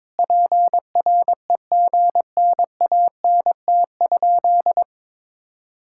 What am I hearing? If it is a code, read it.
Morse code, heard as a keyed tone, about 21 wpm, 700 Hz. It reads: PREGNANT?
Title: Sentences from Top 1000 Words - Review Part 6 - 22wpm